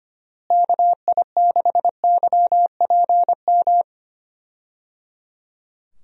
Morse code, heard as a keyed tone, about 25 wpm, 700 Hz. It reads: KI6YPM
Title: USA Call Signs - 6 Characters - Repeated 3x - 25wpm